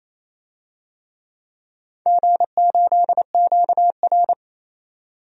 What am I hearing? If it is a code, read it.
Morse code, heard as a keyed tone, about 29 wpm, 700 Hz. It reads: G8QR
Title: DXCC-100 Call Signs - 4 Characters - 28wpm